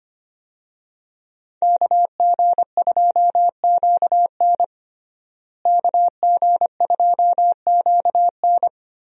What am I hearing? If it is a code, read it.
Morse code, heard as a keyed tone, about 25 wpm, 700 Hz. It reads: KG2QN KG2QN
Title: USA Call Signs - 5 Characters - Repeated 2x - 25wpm